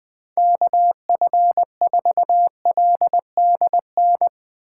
Morse code, heard as a keyed tone, about 20 wpm, 700 Hz. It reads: KF4LDN
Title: Speed Racing - USA 2x3 Call Signs 17wpm